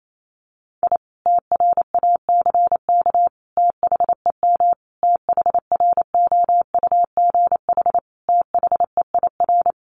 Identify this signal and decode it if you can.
Morse code, heard as a keyed tone, about 28 wpm, 700 Hz. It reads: I TRACK THEM THROUGH THEIR
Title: A Sound Of Thunder 28wpm